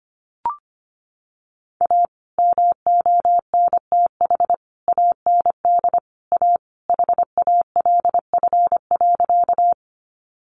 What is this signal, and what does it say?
Morse code, heard as a keyed tone, about 25 wpm, 700 Hz. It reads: A MONTH AND A HALF.
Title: Sentences from Top 800 Words 25wpm